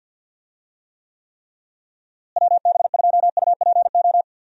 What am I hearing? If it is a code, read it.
Morse code, heard as a keyed tone, about 46 wpm, 700 Hz. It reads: WB2UPO
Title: USA Call Signs - 6 Characters - Repeated 3x - 50wpm